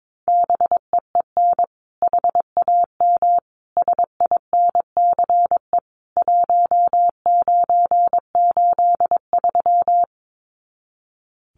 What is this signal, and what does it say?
Morse code, heard as a keyed tone, about 22 wpm, 700 Hz. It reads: BEEN HAM SINCE 1983
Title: Common QSO Phrases 22wpm